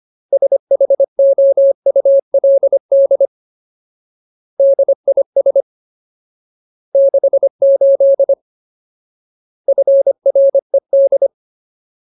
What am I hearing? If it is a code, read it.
Morse code, heard as a keyed tone, about 25 wpm, 550 Hz. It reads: SHOULD DIS 68 FRED
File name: Ultimate Word List / 25WPM Ultimate